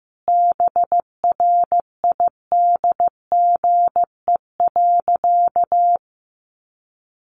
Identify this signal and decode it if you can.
Morse code, heard as a keyed tone, about 15 wpm, 700 Hz. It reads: BRIDGE.
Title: Sentences from Top 1200 Words 15wpm